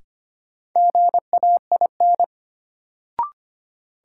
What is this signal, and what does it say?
Morse code, heard as a keyed tone, about 25 wpm, 700 Hz. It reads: GAIN
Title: Sentences from Top 1000 Words - New Word Drill - 25wpm